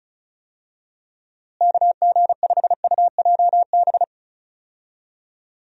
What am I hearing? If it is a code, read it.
Morse code, heard as a keyed tone, about 34 wpm, 700 Hz. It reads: KG5UJB
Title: USA Call Signs - 6 Characters - Repeated 3x - 35wpm